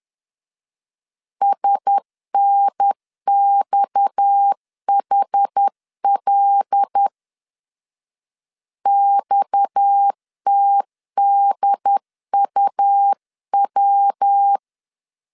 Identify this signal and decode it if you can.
Morse code, heard as a keyed tone, about 10 wpm, 800 Hz. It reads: SNXHL XTDUW
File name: CW / DET / 1 6 1500C S C 10wpm 4252